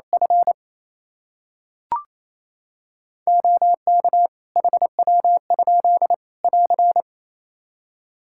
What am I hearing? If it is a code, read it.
Morse code, heard as a keyed tone, about 28 wpm, 700 Hz. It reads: F OK HW? <AR>
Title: Practice QSO - Exchange 2 - Info - 28wpm